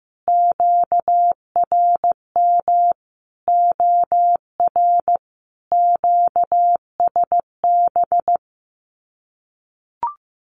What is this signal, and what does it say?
Morse code, heard as a keyed tone, about 15 wpm, 700 Hz. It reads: QRM OR QSB
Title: Common QSO Phrases 15wpm